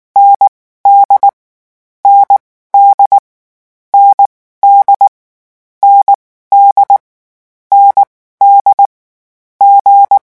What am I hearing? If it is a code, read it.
Morse code, heard as a keyed tone, about 19 wpm, spaced as for 14 wpm, 800 Hz. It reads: ND ND ND ND ND G
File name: morse mp3 / lecon4-vitesse 18